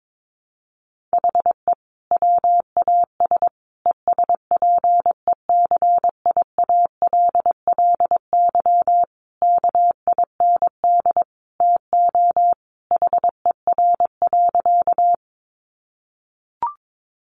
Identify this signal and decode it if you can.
Morse code, heard as a keyed tone, about 22 wpm, 700 Hz. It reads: HE WAS ESPECIALLY KIND TO HER.
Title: Sentences from Top 700 Words 22wpm